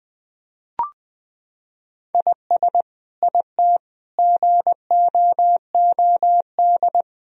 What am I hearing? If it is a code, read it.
Morse code, heard as a keyed tone, about 21 wpm, 700 Hz. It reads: IS IT GOOD
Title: Sentences from Top 1000 Words - Review Part 4 - 20wpm